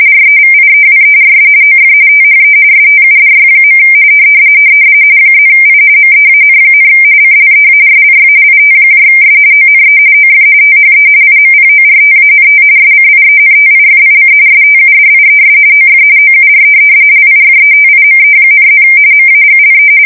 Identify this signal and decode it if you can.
RTTY, about 45.45 baud. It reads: REVIEWERS OF THE STUDY. +I MEAN, CLEARLY ITS ONE OF THE IMPORTANT SCIENTIFIC STUDIES OF OUR LIFETIME,+ MAGLE
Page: 170 Shift 2125/2295